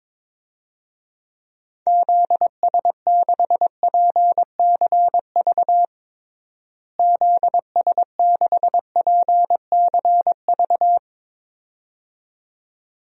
Morse code, heard as a keyed tone, about 21 wpm, 700 Hz. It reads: ZS6PCV ZS6PCV
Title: DXCC-100 Call Signs - 6 Characters - Repeated 2x - 22wpm